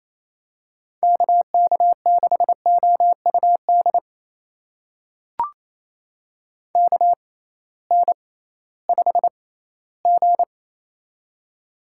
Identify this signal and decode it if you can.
Morse code, heard as a keyed tone, about 27 wpm, spaced as for 16 wpm, 700 Hz. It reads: KK6OUD KN5G
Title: USA Call Signs - 6x Character Spacing - 28wpm